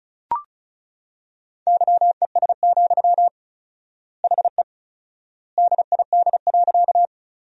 Morse code, heard as a keyed tone, about 34 wpm, 700 Hz. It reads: YES, HE DID.
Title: 4x Word-Spacing - Sentences from Top 500 Words - 35wpm